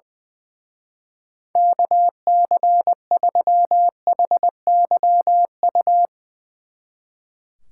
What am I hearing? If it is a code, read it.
Morse code, heard as a keyed tone, about 20 wpm, 700 Hz. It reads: KC3HYU
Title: Speed Racing - USA 2x3 Call Signs 20wpm